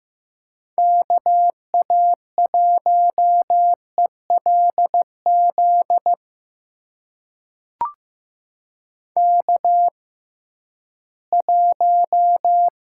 Morse code, heard as a keyed tone, about 15 wpm, spaced as for 11 wpm, 700 Hz. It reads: KA1ELZ K1
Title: USA Call Signs - 6x Character Spacing - 15wpm